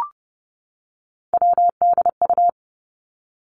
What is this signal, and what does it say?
Morse code, heard as a keyed tone, about 30 wpm, 700 Hz. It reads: WDU
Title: Three Challenging Letter-Number Combinations 30wpm